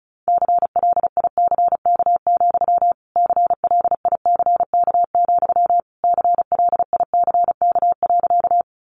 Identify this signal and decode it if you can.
Morse code, heard as a keyed tone, about 34 wpm, 700 Hz. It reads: CLICK, CLICK, CLICK.
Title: Sentences from Top 1100 Words 35wpm